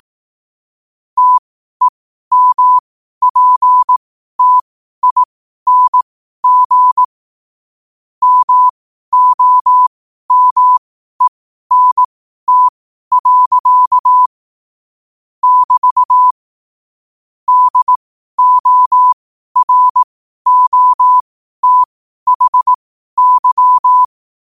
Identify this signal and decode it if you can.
Morse code, heard as a keyed tone, about 18 wpm, spaced as for 12 wpm, 1000 Hz. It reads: TEMPTING MOMENT. <BT> DOROTHY
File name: qotdfiles / 12wpm